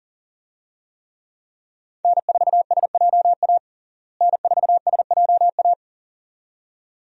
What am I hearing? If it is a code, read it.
Morse code, heard as a keyed tone, about 40 wpm, 700 Hz. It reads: N4SJA N4SJA
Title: USA Call Signs - 5 Characters - Repeated 2x - 40wpm